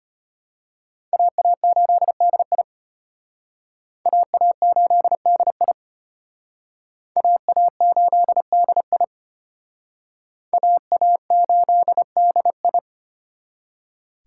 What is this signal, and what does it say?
Morse code, heard as a keyed tone, about 34 wpm, 700 Hz. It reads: AA8DI AA8DI AA8DI AA8DI